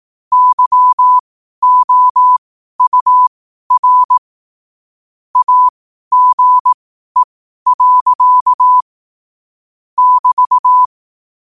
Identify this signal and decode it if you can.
Morse code, heard as a keyed tone, about 18 wpm, spaced as for 12 wpm, 1000 Hz. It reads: YOUR AGE. <BT>
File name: qotdfiles / 12wpm